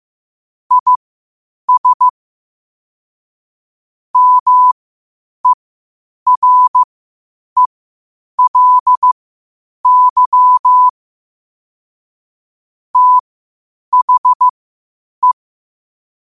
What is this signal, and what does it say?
Morse code, heard as a keyed tone, about 15 wpm, spaced as for 8 wpm, 1000 Hz. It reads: IS MERELY THE
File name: qotdfiles / 8wpm